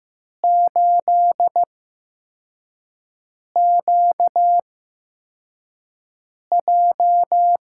Morse code, heard as a keyed tone, about 15 wpm, spaced as for 4 wpm, 700 Hz. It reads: 8QJ
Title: USA Call Signs - 8x Character Spacing - 15wpm